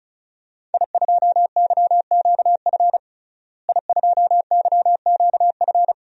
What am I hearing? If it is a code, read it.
Morse code, heard as a keyed tone, about 36 wpm, 700 Hz. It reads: I2YQF I2YQF
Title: DXCC-100 Call Signs - 5 Characters - Repeated 3x - 35wpm